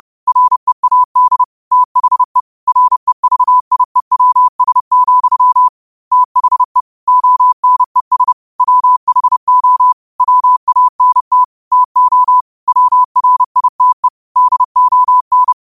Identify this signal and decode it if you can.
Morse code, heard as a keyed tone, about 30 wpm, 1000 Hz. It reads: READ THE REVIEWS, THE ONES WHO WANT TO WRITE DON
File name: qotdfiles / 30wpm